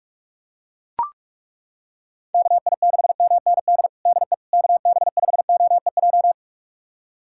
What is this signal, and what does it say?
Morse code, heard as a keyed tone, about 41 wpm, 700 Hz. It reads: KI6MND DE KB5OEJ
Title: USA Callsign DE Callsign 45wpm